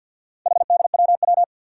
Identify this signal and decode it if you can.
Morse code, heard as a keyed tone, about 49 wpm, 700 Hz. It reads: HDWW
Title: Four Challenging Letter-Number Combinations 50wpm